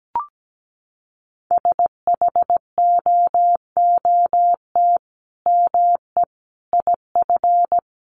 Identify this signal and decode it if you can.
Morse code, heard as a keyed tone, about 17 wpm, 700 Hz. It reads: SHOOT ME IF
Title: Sentences from Top 1200 Words 17wpm